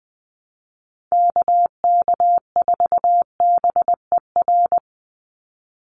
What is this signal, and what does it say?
Morse code, heard as a keyed tone, about 20 wpm, 700 Hz. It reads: KK4BER
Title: USA Call Signs 20wpm